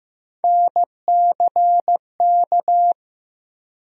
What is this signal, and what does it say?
Morse code, heard as a keyed tone, about 15 wpm, 700 Hz. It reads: NCK
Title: USA Call Sign Postfixes - 3 Letters 15wpm